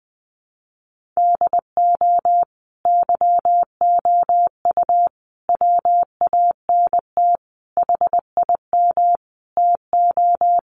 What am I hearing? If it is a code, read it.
Morse code, heard as a keyed tone, about 20 wpm, 700 Hz. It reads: DO YOU WANT HIM TO
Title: Sentences from Top 1400 Words 20wpm